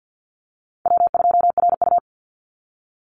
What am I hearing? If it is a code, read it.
Morse code, heard as a keyed tone, about 50 wpm, 700 Hz. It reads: W2RU
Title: Easy Call Signs Repeated 2x - 50wpm